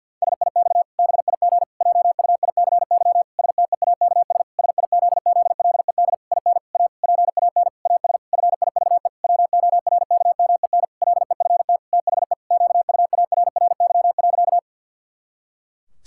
Morse code, heard as a keyed tone, about 51 wpm, 700 Hz. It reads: SIX BIG JUICY STEAKS SIZZLED IN A PAN AS FIVE WORKMEN LEFT THE QUARRY.
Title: Pangram Challenge v2 - 50wpm